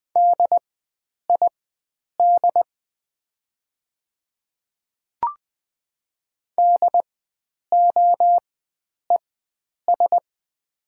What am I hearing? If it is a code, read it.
Morse code, heard as a keyed tone, about 20 wpm, spaced as for 9 wpm, 700 Hz. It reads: DID DOES
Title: 4x Character Spacing - Top 500 Words - 20wpm